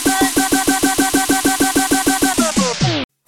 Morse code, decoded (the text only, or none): none